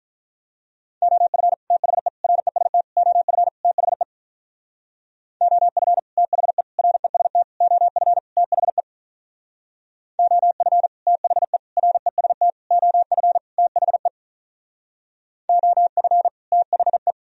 Morse code, heard as a keyed tone, about 43 wpm, 700 Hz. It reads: OF THE REST OF THE OF THE REST OF THE OF THE REST OF THE OF THE
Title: Speed-Racing - Sets of 5 Words 35wpm